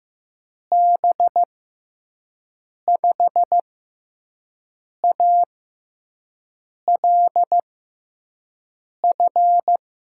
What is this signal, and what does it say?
Morse code, heard as a keyed tone, about 15 wpm, spaced as for 5 wpm, 700 Hz. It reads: B5ALF
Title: USA Call Signs - 6 Characters - 6x Character Spacing - 15wpm